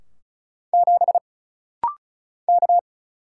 Morse code, heard as a keyed tone, about 36 wpm, 700 Hz. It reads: 7 K
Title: Single Letter-Number Alt - Rapid-Fire 35wpm